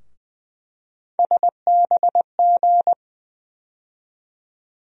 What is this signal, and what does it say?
Morse code, heard as a keyed tone, about 20 wpm, 700 Hz. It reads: SBG
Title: USA Call Sign Postfixes - 3 Letters 20wpm